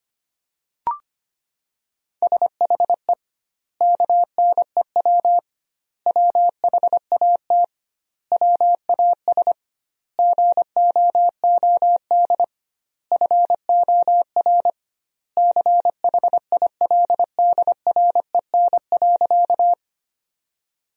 Morse code, heard as a keyed tone, about 25 wpm, 700 Hz. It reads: SHE KNEW WHAT WAS GOOD FOR CHILDREN.
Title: 2x Word-Spacing - Sentences from Top 400 Words - 25wpm